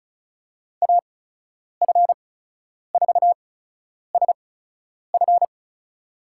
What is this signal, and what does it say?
Morse code, heard as a keyed tone, about 34 wpm, spaced as for 10 wpm, 700 Hz. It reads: AF4SF
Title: USA Call Signs - 5 Characters - 8x Character Spacing - 35wpm